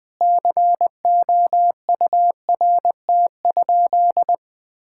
Morse code, heard as a keyed tone, about 20 wpm, 700 Hz. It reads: COURT?
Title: Sentences from Top 1600 Words 20wpm